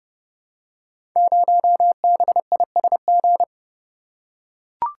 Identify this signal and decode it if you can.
Morse code, heard as a keyed tone, about 30 wpm, 700 Hz. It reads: 0BISG
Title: Code Groups - 5 Letter:Numbers - 30wpm